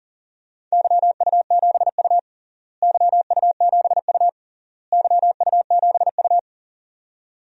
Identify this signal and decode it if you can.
Morse code, heard as a keyed tone, about 40 wpm, 700 Hz. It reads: YU7U YU7U YU7U
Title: DXCC-100 Call Signs - 4 Characters - Repeated 3x - 40wpm